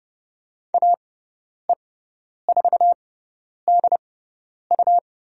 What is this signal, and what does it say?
Morse code, heard as a keyed tone, about 30 wpm, spaced as for 10 wpm, 700 Hz. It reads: AE4DU
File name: Lesson 17g - Letter u - Call Signs 30wpm